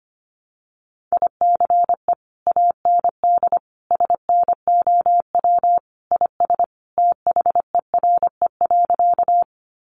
Morse code, heard as a keyed tone, about 25 wpm, 700 Hz. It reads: ICE AND SNOW IS THERE.
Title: Sentences from Top 1000 Words 25wpm